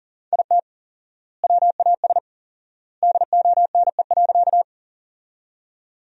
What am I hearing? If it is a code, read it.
Morse code, heard as a keyed tone, about 40 wpm, 700 Hz. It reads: IT WAS DONE.
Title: 4x Word-Spacing - Sentences from Top 500 Words - 40wpm